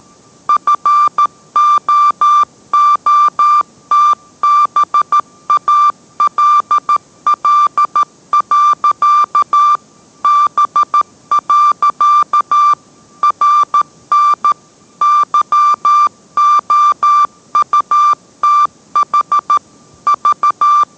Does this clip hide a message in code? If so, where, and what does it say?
Yes: Morse code from 0.5 to 20.8 s, heard as a keyed tone, about 13 wpm, 1180 Hz. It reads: FOOTBALL. B. RN YOUTH V